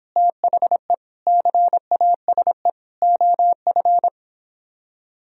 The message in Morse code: THE CASE OF

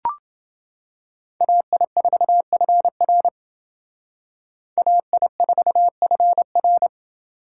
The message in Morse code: AI4FR AI4FR

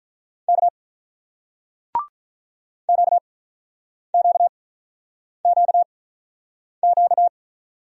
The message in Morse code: K QQQQ

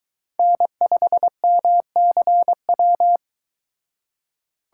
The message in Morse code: N5MCW

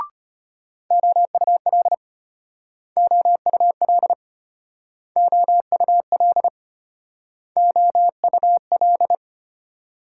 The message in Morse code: OUL OUL OUL OUL

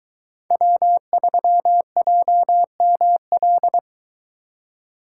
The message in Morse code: W3JML

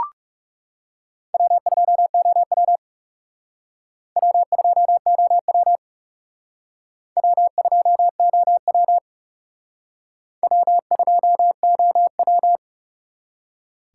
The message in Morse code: W2OW W2OW W2OW W2OW

